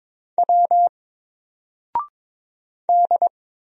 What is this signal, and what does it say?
Morse code, heard as a keyed tone, about 22 wpm, 700 Hz. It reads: W D
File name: Single-Letter-Number-22wpm